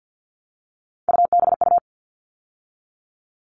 Morse code, heard as a keyed tone, about 50 wpm, 700 Hz. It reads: UBU
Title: All Cut Numbers - 3-Digit 50wpm